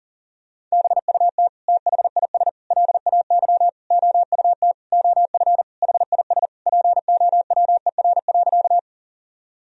Morse code, heard as a keyed tone, about 40 wpm, 700 Hz. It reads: BUT THIS LAY OUT OF HIS POWER.
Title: Sentences from Top 1000 Words - Review Part 2 - 40wpm